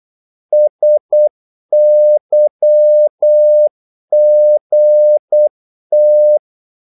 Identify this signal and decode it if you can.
Morse code, heard as a keyed tone, about 8 wpm, 600 Hz. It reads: SYGT
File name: Korte ord DK 8wpm